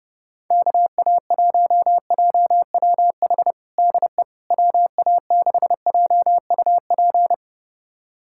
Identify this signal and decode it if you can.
Morse code, heard as a keyed tone, about 30 wpm, 700 Hz. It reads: KA1JWH DE WA6JUP
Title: USA Callsign DE Callsign - 6 Characters - 30wpm